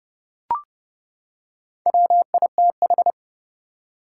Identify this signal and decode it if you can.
Morse code, heard as a keyed tone, about 28 wpm, 700 Hz. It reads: WITH
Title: Top 180 QSO Elements 30wpm